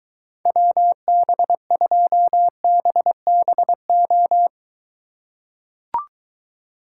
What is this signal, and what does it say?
Morse code, heard as a keyed tone, about 23 wpm, 700 Hz. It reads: WB2BBO